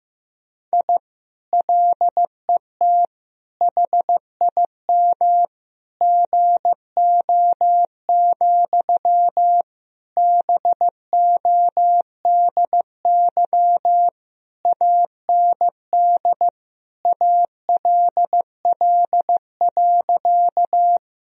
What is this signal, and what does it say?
Morse code, heard as a keyed tone, about 15 wpm, 700 Hz. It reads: I LET HIM GO, BODY AND ALL.
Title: Sentences from Top 1000 Words - Review Part 2 - 15wpm